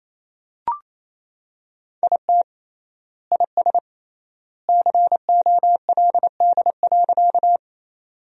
Morse code, heard as a keyed tone, about 28 wpm, 700 Hz. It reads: IT IS COLD.
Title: 3x Word-Spacing - Sentences from Top 400 Words - 28wpm